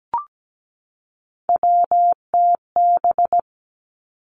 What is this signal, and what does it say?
Morse code, heard as a keyed tone, about 17 wpm, 700 Hz. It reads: WTB
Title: Three Challenging Letter-Number Combinations 17wpm